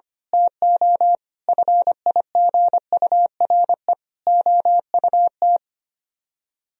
Morse code, heard as a keyed tone, about 24 wpm, 700 Hz. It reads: TO FIGURE OUT
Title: Speed-Racing - Sets of 4 Words 25wpm